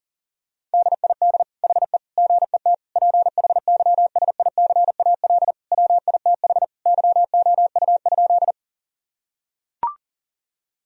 Morse code, heard as a keyed tone, about 40 wpm, 700 Hz. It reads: DID HE GET PHYSICAL WITH YOU?
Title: Sentences from Top 1000 Words - Review Part 4 - 40wpm